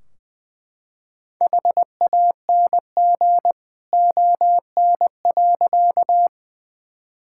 Morse code, heard as a keyed tone, about 20 wpm, 700 Hz. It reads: HANG ON.
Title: Sentences from Top 1600 Words 20wpm